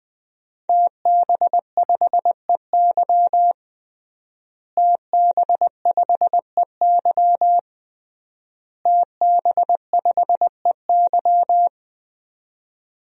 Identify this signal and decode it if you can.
Morse code, heard as a keyed tone, about 20 wpm, 700 Hz. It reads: TB5EY TB5EY TB5EY
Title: DXCC-100 Call Signs - 5 Characters - Repeated 3x - 20wpm